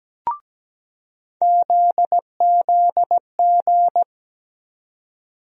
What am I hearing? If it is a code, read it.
Morse code, heard as a keyed tone, about 17 wpm, 700 Hz. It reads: ZZG
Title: USA Call Sign Postfixes - 3 Letters 17wpm